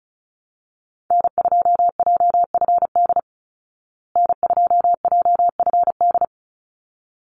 Morse code, heard as a keyed tone, about 34 wpm, 700 Hz. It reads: N2JFD N2JFD
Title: Call Signs Repeated 3x - 35wpm